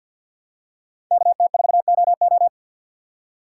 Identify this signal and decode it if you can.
Morse code, heard as a keyed tone, about 51 wpm, 700 Hz. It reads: KT4OO
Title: USA Call Signs - 5 Characters - 3x Character Spacing - 50wpm